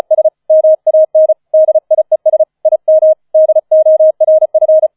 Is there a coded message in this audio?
Yes: Morse code from 0.1 to 4.9 s, heard as a keyed tone, about 36 wpm, 630 Hz. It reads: S MAN DIES IM DORF